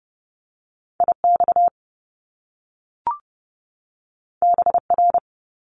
Morse code, heard as a keyed tone, about 30 wpm, 700 Hz. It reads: IX BR
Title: Top Two Letter Combinations in Words 30wpm